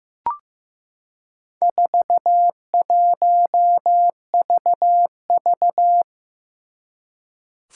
Morse code, heard as a keyed tone, about 15 wpm, 700 Hz. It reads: 41VV
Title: Four Challenging Letter-Number Combinations 15wpm